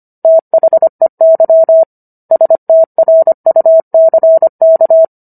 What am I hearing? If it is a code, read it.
Morse code, heard as a keyed tone, about 25 wpm, 650 Hz. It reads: THEY STRUCK